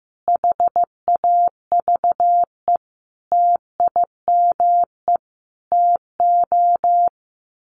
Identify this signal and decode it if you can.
Morse code, heard as a keyed tone, about 15 wpm, 700 Hz. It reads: HAVE TIME TO
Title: Sets of 5 Words 15wpm